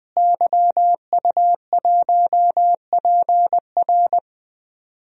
Morse code, heard as a keyed tone, about 20 wpm, 700 Hz. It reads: YU1PR
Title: DXCC-100 Call Signs - 5 Characters - Repeated 2x - 20wpm